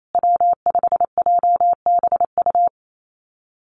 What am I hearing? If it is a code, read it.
Morse code, heard as a keyed tone, about 28 wpm, 700 Hz. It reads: W5JBU
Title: USA Call Signs Repeated 3x 28wpm